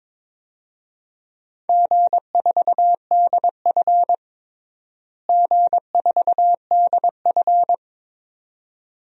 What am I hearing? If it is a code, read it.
Morse code, heard as a keyed tone, about 22 wpm, 700 Hz. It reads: G4DF G4DF
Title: DXCC-100 Call Signs - 4 Characters - Repeated 2x - 22wpm